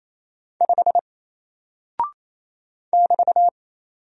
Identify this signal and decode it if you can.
Morse code, heard as a keyed tone, about 28 wpm, 700 Hz. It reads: 5 <BT>
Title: Single Character (Letter:Number:Punctuation:Prosign) Alt 28wpm